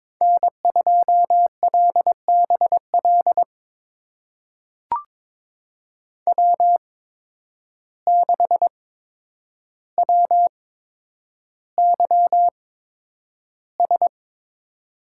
Morse code, heard as a keyed tone, about 21 wpm, spaced as for 9 wpm, 700 Hz. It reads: N2LBL W6WYS
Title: USA Call Signs - 8x Character Spacing - 22wpm